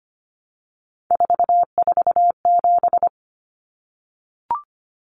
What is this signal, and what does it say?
Morse code, heard as a keyed tone, about 25 wpm, 700 Hz. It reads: 447
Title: Common Cut Numbers - 3-Digit 25wpm